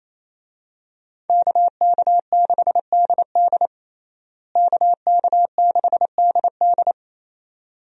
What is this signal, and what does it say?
Morse code, heard as a keyed tone, about 28 wpm, 700 Hz. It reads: KK6DD KK6DD
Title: USA Call Signs - 5 Characters - Repeated 2x - 28wpm